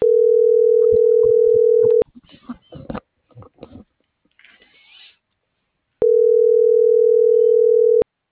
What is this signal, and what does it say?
Ringback tone (North America), 440 + 480 Hz.
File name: CENTRAL MORENA / out jacque.aguilera.25 - .28001